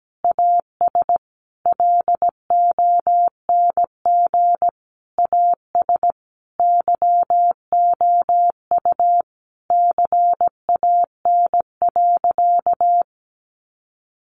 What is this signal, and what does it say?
Morse code, heard as a keyed tone, about 17 wpm, 700 Hz. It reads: AS LONG AS YOU CAN.
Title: Sentences from Top 1200 Words 17wpm